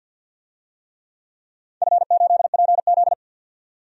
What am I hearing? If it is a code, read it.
Morse code, heard as a keyed tone, about 47 wpm, 700 Hz. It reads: F8PZ
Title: DXCC-100 Call Signs - 4 Characters - 50wpm